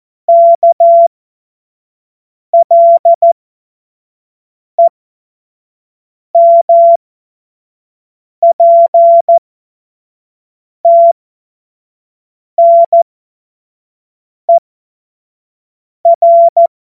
Morse code, heard as a keyed tone, about 14 wpm, spaced as for 5 wpm, 680 Hz. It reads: KLEMPTNER
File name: Morsekurs / Witze / Witz 16. Dezember 2025 25 F70